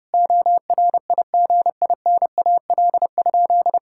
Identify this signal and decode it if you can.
Morse code, heard as a keyed tone, about 30 wpm, 700 Hz. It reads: ORIGINAL?
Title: Sentences from Top 1000 Words - Review Part 6 - 30wpm